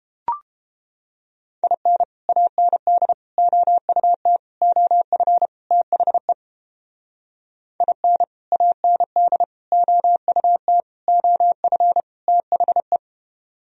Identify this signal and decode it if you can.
Morse code, heard as a keyed tone, about 30 wpm, 700 Hz. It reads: IN AND OUT OF THE IN AND OUT OF THE